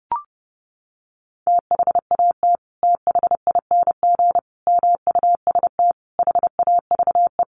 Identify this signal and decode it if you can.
Morse code, heard as a keyed tone, about 30 wpm, 700 Hz. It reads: THAT THING MUST HAVE
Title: Sentences from Top 1500 Words 30wpm